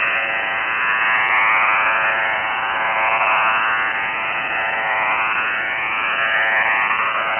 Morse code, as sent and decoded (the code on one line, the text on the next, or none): none